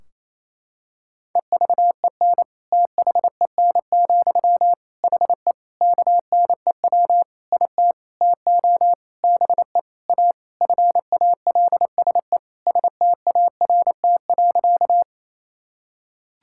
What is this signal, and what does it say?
Morse code, heard as a keyed tone, about 28 wpm, 700 Hz. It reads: EVEN THEN, HE KNEW IT TO BE A FALSE START.
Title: Sentences from Top 1600 Words 28wpm